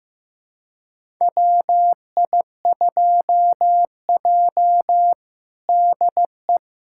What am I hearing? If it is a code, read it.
Morse code, heard as a keyed tone, about 15 wpm, 700 Hz. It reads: WI2J DE
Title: USA Callsign DE Callsign - 4 Characters - 15wpm